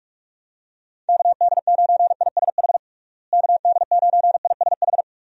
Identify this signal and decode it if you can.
Morse code, heard as a keyed tone, about 46 wpm, 700 Hz. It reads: KD9ISH KD9ISH
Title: USA Call Signs - 6 Characters - Repeated 3x - 45wpm